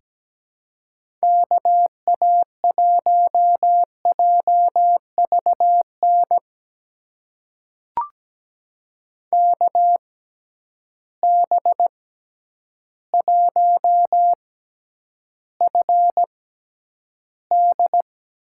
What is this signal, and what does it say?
Morse code, heard as a keyed tone, about 17 wpm, spaced as for 9 wpm, 700 Hz. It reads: KA1JVN KB1FD
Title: USA Call Signs - 6 Characters - 6x Character Spacing - 17wpm